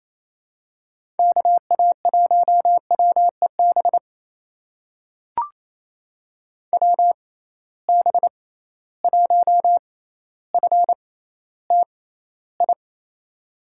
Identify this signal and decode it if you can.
Morse code, heard as a keyed tone, about 28 wpm, spaced as for 14 wpm, 700 Hz. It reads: KA1WEB WB1FTI
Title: USA Call Signs - 6 Characters - 6x Character Spacing - 28wpm